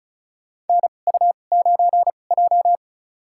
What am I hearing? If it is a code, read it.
Morse code, heard as a keyed tone, about 36 wpm, spaced as for 25 wpm, 700 Hz. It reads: NU9J
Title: USA Call Signs - 2x Character Spacing - 35wpm